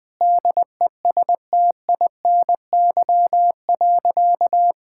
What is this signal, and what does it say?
Morse code, heard as a keyed tone, about 20 wpm, 700 Hz. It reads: DESTINY.
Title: Sentences with 6-Word Phrases - Spoken Beginning - 20wpm